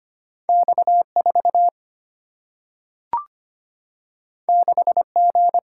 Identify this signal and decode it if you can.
Morse code, heard as a keyed tone, about 25 wpm, 700 Hz. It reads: X 4 6 G
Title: Single-Letter Single-Number 25wpm